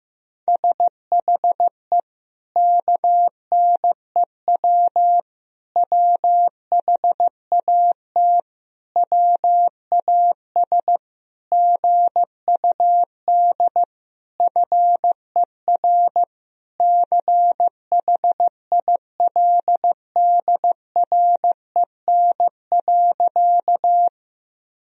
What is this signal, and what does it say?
Morse code, heard as a keyed tone, about 16 wpm, 700 Hz. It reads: SHE KNEW WHAT WAS GUD FER CHILDREN.
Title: Sentences from Top 400 Words and Abbreviations 15wpm